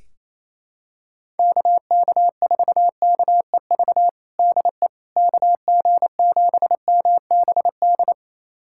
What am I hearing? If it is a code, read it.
Morse code, heard as a keyed tone, about 28 wpm, 700 Hz. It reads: KK4KEV DE KG7MBD
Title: USA Callsign DE Callsign - 6 Characters - 28wpm